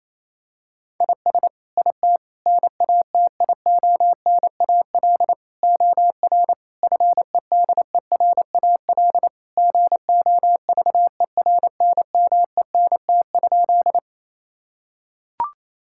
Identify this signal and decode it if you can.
Morse code, heard as a keyed tone, about 28 wpm, 700 Hz. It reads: IS IT NATIONAL OR FEDERAL GOVERNMENT?
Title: Sentences from Top 1000 Words - Review Part 5 - 28wpm